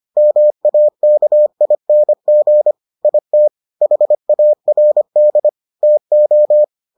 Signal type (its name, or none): Morse code